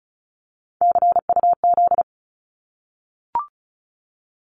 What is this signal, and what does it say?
Morse code, heard as a keyed tone, about 36 wpm, 700 Hz. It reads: CUZ
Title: Abbreviations 35wpm